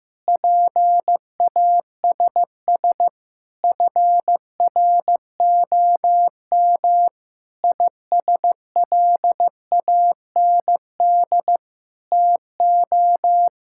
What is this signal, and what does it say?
Morse code, heard as a keyed tone, about 15 wpm, 700 Hz. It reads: PASS FROM ISLAND TO
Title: Sentences from Top 1000 Words - Review Part 6 - 15wpm